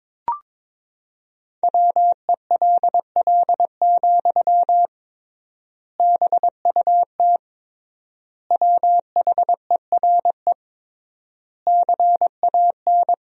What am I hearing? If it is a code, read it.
Morse code, heard as a keyed tone, about 22 wpm, 700 Hz. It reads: WELL, BUT WHERE CAN
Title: 3x Word-Spacing - Sentences from Top 500 Words - 22wpm